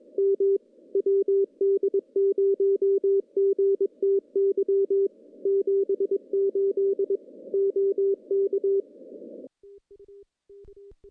Morse code, heard as a keyed tone, about 22 wpm, 390 Hz. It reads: M WD0GTY 78 OK
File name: contests / CW-SS-k5tr- -1500